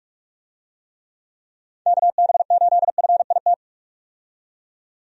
Morse code, heard as a keyed tone, about 46 wpm, 700 Hz. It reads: KB8FIT
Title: USA Call Signs - 6 Characters - 45wpm